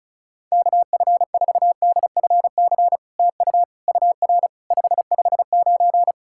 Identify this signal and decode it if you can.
Morse code, heard as a keyed tone, about 36 wpm, 690 Hz. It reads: KF4DFC TU UR 559
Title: SOTA QSO Practice 35wpm